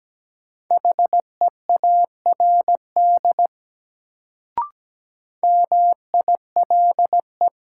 Morse code, heard as a keyed tone, about 17 wpm, 700 Hz. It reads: HEARD MILE
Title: Sentences from Top 400 Words - New Word Drill - 17wpm